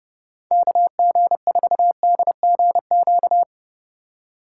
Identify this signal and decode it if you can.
Morse code, heard as a keyed tone, about 37 wpm, 700 Hz. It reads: KG4DGQ
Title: Speed Racing - USA Call Signs 30wpm